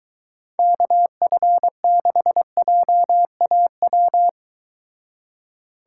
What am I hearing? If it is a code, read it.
Morse code, heard as a keyed tone, about 23 wpm, 700 Hz. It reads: KF6JAW